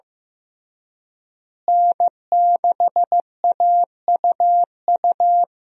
Morse code, heard as a keyed tone, about 16 wpm, 700 Hz. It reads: N6AUU
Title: USA Call Signs - 5 Characters - Repeated 3x - 15wpm